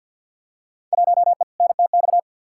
Morse code, heard as a keyed tone, about 53 wpm, 700 Hz. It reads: 1E NTX
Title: Field Day Exchanges - 50wpm